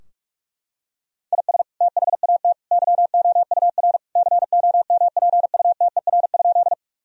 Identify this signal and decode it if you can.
Morse code, heard as a keyed tone, about 42 wpm, 700 Hz. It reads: IS THAT YOUR COMPUTER?
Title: Sentences from Top 1000 Words - Review Part 3 - 45wpm